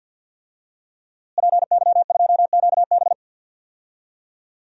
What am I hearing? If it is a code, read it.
Morse code, heard as a keyed tone, about 51 wpm, 700 Hz. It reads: PY2QB
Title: DXCC-100 Call Signs - 5 Characters - Repeated 2x - 50wpm